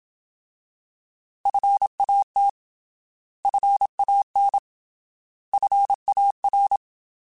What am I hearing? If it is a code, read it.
Morse code, heard as a keyed tone, about 27 wpm, 800 Hz. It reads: FAT FAN FAR
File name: Alphabetical / Alphabetical mp3s / Fs @25